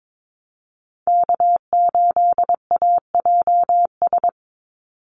Morse code, heard as a keyed tone, about 22 wpm, 700 Hz. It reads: K8AJS
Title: Call Signs - 22wpm